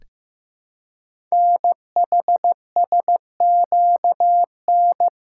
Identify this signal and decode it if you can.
Morse code, heard as a keyed tone, about 15 wpm, 700 Hz. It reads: NHSQN
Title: Code Groups - 5 Letter:Numbers - 15wpm